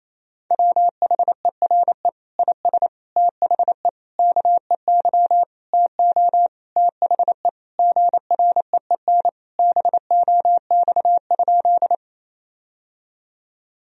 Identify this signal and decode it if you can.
Morse code, heard as a keyed tone, about 28 wpm, 700 Hz. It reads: WHERE IS THE KEY TO THE GREEN BOX?
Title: Sentences from Top 1000 Words - Review Part 4 - 28wpm